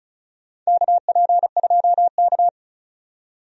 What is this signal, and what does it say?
Morse code, heard as a keyed tone, about 36 wpm, 700 Hz. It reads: KP2K
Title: DXCC-100 Call Signs - 4 Characters - 35wpm